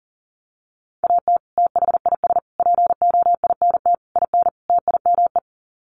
Morse code, heard as a keyed tone, about 40 wpm, 700 Hz. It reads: AT THIS POINT IN TIME
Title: Sets of 5 Words Encore 40wpm